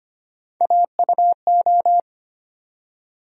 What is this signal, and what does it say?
Morse code, heard as a keyed tone, about 25 wpm, 700 Hz. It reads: AUO
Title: USA Call Sign Postfixes - 3 Letters 25wpm